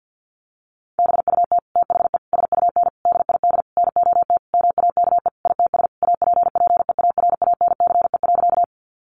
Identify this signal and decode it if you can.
Morse code, heard as a keyed tone, about 50 wpm, 700 Hz. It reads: BUT THE SUN DID NOT MAKE ITS APPEARANCE.
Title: Sentences from Top 1100 Words 50wpm